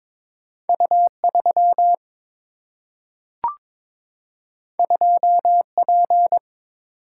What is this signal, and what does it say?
Morse code, heard as a keyed tone, about 22 wpm, 700 Hz. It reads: U 3 2 P
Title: Single-Letter Single-Number 22wpm